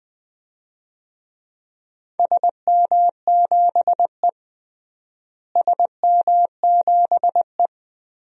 Morse code, heard as a keyed tone, about 20 wpm, 700 Hz. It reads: SM7E SM7E
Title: DXCC-100 Call Signs - 4 Characters - Repeated 3x - 20wpm